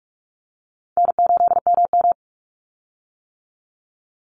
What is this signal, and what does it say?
Morse code, heard as a keyed tone, about 44 wpm, 700 Hz. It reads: N8MM
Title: USA Call Signs 45wpm